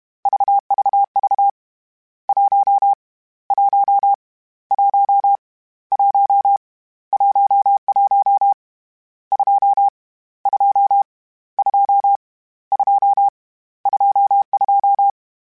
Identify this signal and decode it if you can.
Morse code, heard as a keyed tone, about 31 wpm, 800 Hz. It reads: VVV 1 1 1 1 11 2 2 2 2 22